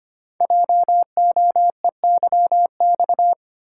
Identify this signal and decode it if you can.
Morse code, heard as a keyed tone, about 26 wpm, 700 Hz. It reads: JOEYX
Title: Code Groups - 5 Letter:Numbers - 25wpm